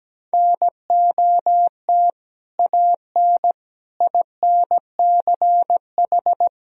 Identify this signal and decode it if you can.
Morse code, heard as a keyed tone, about 17 wpm, 700 Hz. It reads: NOT AN INCH
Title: Sentences from Top 1000 Words - Review Part 3 - 17wpm